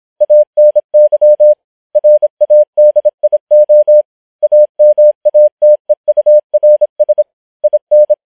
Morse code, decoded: ANY RADIO AMATEURS IN